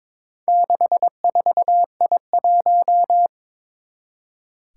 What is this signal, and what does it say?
Morse code, heard as a keyed tone, about 22 wpm, 700 Hz. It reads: 64I1
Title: Four Challenging Letter-Number Combinations 22wpm